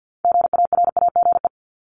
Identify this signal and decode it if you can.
Morse code, heard as a keyed tone, about 50 wpm, 700 Hz. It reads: GARAGE
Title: Multi-letter Phonemes 50wpm